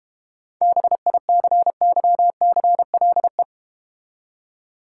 Morse code, heard as a keyed tone, about 32 wpm, 700 Hz. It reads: BICYCLE